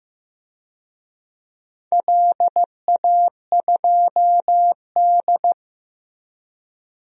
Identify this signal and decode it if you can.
Morse code, heard as a keyed tone, about 15 wpm, 700 Hz. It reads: LA2D
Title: DXCC-100 Call Signs - 4 Characters - Repeated 3x - 15wpm